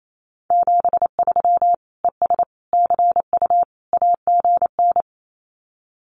Morse code, heard as a keyed tone, about 28 wpm, 700 Hz. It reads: 73 ES CU AGN
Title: Common QSO Phrases Encore 28wpm